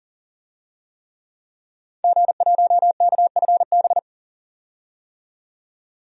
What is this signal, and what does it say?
Morse code, heard as a keyed tone, about 40 wpm, 700 Hz. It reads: G1KFB
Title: DXCC-100 Call Signs - 5 Characters - 40wpm